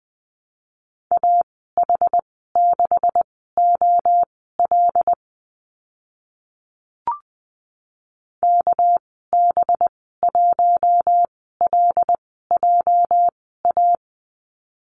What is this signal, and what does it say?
Morse code, heard as a keyed tone, about 20 wpm, spaced as for 14 wpm, 700 Hz. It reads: AH6OL KB1LJA
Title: 207 Lesson 33g - Letter j - Call Signs 20wpm